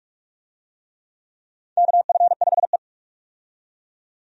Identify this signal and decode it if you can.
Morse code, heard as a keyed tone, about 46 wpm, 700 Hz. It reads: KF5E
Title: USA Call Signs - 4 Characters - 45wpm